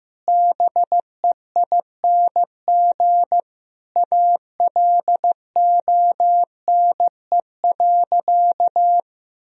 Morse code, heard as a keyed tone, about 15 wpm, 700 Hz. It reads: BEING ALONE.
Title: Sentences from Top 1000 Words - Review Part 3 - 15wpm